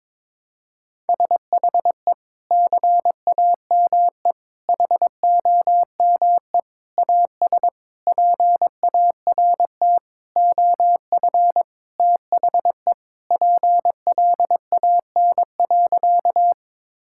Morse code, heard as a keyed tone, about 22 wpm, 700 Hz. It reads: SHE CAME HOME AS PART OF THE PLAN.
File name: practice / 22wpm / Sentences with 4-Word Phrases - 22wpm